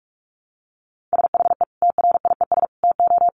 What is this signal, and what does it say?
Morse code, heard as a keyed tone, about 44 wpm, 700 Hz. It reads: SHE TRIES TO
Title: Easy Phrases 45wpm